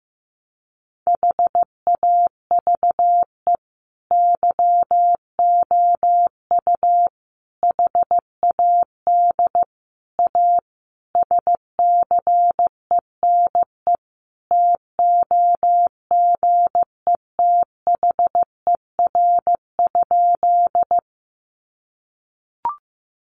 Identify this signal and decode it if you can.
Morse code, heard as a keyed tone, about 15 wpm, 700 Hz. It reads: HAVE YOU HAD A SCENE TOGETHER?
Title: Sentences from Top 1100 Words 15wpm